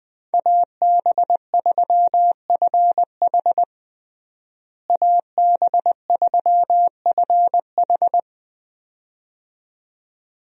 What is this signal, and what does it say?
Morse code, heard as a keyed tone, about 21 wpm, 700 Hz. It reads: AB3FH AB3FH
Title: USA Call Signs - 5 Characters - Repeated 2x - 20wpm